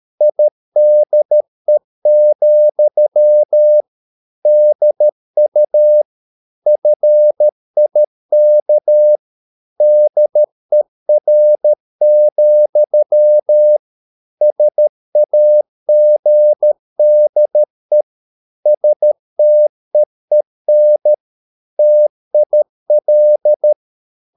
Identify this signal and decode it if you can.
Morse code, heard as a keyed tone, about 13 wpm, 600 Hz. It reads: IDE, DU FIK DER, SAGDE STEEN TIL
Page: Ferie 13 wpm